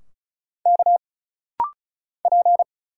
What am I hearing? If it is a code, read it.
Morse code, heard as a keyed tone, about 35 wpm, 700 Hz. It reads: K P
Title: Single Letters - Rapid-Fire 35wpm